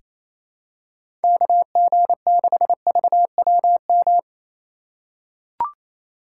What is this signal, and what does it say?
Morse code, heard as a keyed tone, about 28 wpm, 700 Hz. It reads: KG6VWM
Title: USA Call Signs - 3x Character Spacing - 28wpm